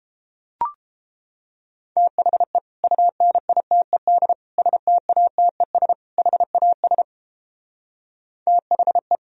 Morse code, heard as a keyed tone, about 29 wpm, 700 Hz. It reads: THE UNITED STATES HAS THE